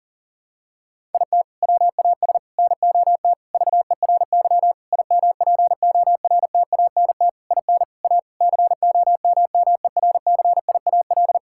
Morse code, heard as a keyed tone, about 40 wpm, 700 Hz. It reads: IT WAS NOT VERY IMPORTANT IN A COMMERCIAL
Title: Sentences from Top 1000 Words - Review Part 5 - 40wpm